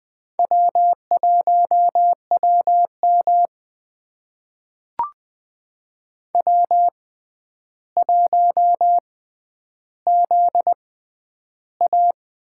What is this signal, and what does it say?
Morse code, heard as a keyed tone, about 20 wpm, spaced as for 10 wpm, 700 Hz. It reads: W1WM W1ZA
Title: USA Call Signs - 4 Characters - 6x Character Spacing - 20wpm